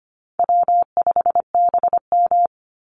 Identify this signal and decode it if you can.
Morse code, heard as a keyed tone, about 25 wpm, 700 Hz. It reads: W5BM
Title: Easy Call Signs Repeated 2x - 25wpm